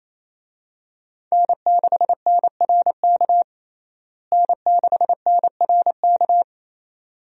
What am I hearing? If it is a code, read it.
Morse code, heard as a keyed tone, about 28 wpm, 700 Hz. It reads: N6NRK N6NRK
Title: USA Call Signs - 5 Characters - Repeated 2x - 28wpm